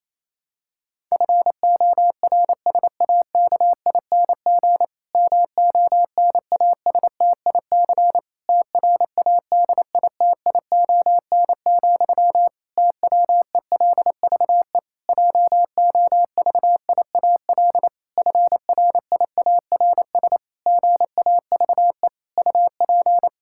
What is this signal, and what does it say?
Morse code, heard as a keyed tone, about 27 wpm, 700 Hz. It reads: FORSAKING MONASTIC TRADITION, TWELVE JOVIAL FRIARS GAVE UP
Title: Pangram Challenge v2 - 28wpm